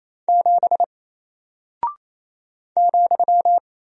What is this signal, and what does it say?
Morse code, heard as a keyed tone, about 28 wpm, 700 Hz. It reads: 7 ,
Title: Single Character (Letter:Number:Punctuation:Prosign) Alt 28wpm